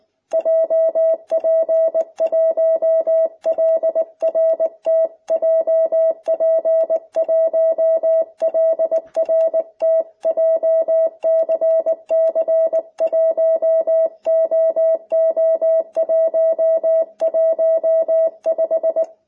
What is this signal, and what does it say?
Morse code, heard as a keyed tone, about 19 wpm, 660 Hz. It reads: JP1LRT JP1LRT JCC1OO115